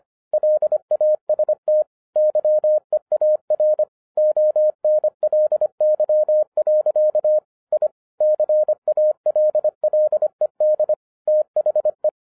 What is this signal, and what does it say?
Morse code, heard as a keyed tone, about 25 wpm, 610 Hz. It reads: LAST YEAR ONLY. I CALLED THE